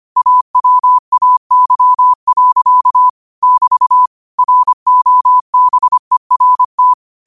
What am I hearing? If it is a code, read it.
Morse code, heard as a keyed tone, about 25 wpm, 1000 Hz. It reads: AWAY. <BT> ROBERT
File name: qotdfiles / 25wpm